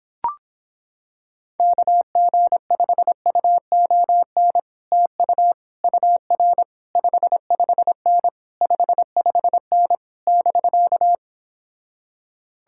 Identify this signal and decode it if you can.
Morse code, heard as a keyed tone, about 26 wpm, 700 Hz. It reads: KG5UON TU UR 55N 55N <BK>